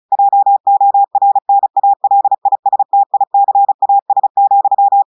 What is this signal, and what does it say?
Morse code, heard as a keyed tone, about 36 wpm, 800 Hz. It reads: JORNALISTICAS,
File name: audios / portugais / portugais texte en clair 2 vitesse 35 mots minutes